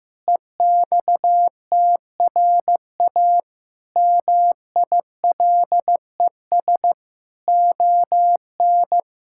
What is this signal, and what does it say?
Morse code, heard as a keyed tone, about 14 wpm, 700 Hz. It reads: EXTRA MILES ON
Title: Pangram Challenge v2 - 15wpm